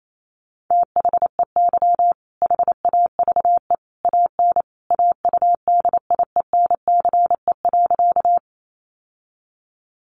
Morse code, heard as a keyed tone, about 28 wpm, 700 Hz. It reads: THEY HAVE AN AUDIENCE.
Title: Sentences from Top 800 Words 28wpm